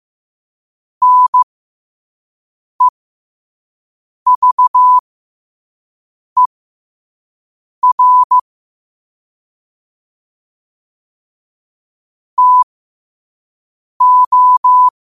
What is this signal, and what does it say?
Morse code, heard as a keyed tone, about 15 wpm, spaced as for 5 wpm, 1000 Hz. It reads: NEVER TO
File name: qotdfiles / 5wpm